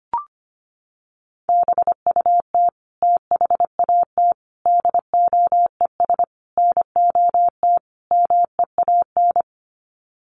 Morse code, heard as a keyed tone, about 25 wpm, 700 Hz. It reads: BUT THAT DOES NOT MEAN
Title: Sets of 5 Words 25wpm